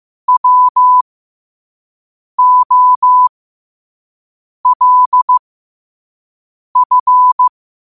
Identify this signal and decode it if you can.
Morse code, heard as a keyed tone, about 15 wpm, spaced as for 5 wpm, 1000 Hz. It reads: WOLF
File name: qotdfiles / 5wpm